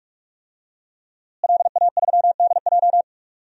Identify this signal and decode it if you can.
Morse code, heard as a keyed tone, about 41 wpm, 700 Hz. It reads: LA3DJ
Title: DXCC-100 Call Signs - 5 Characters - Repeated 2x - 45wpm